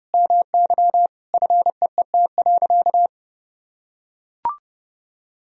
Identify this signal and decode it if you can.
Morse code, heard as a keyed tone, about 30 wpm, 700 Hz. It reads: MY FEET.
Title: Sentences from Top 1000 Words - Review Part 1 - 30wpm